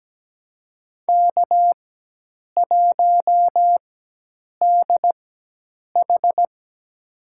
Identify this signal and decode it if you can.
Morse code, heard as a keyed tone, about 17 wpm, spaced as for 8 wpm, 700 Hz. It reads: K1DH
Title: USA Call Signs - 4 Characters - 4x Character Spacing - 17wpm